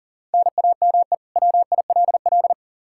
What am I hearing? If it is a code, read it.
Morse code, heard as a keyed tone, about 40 wpm, 700 Hz. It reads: NAME WILL
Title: K1USN SST QSO Practice 40wpm